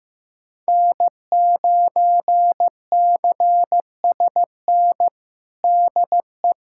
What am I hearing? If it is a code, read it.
Morse code, heard as a keyed tone, about 15 wpm, 700 Hz. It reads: N9CSN DE
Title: USA Callsign DE Callsign 15wpm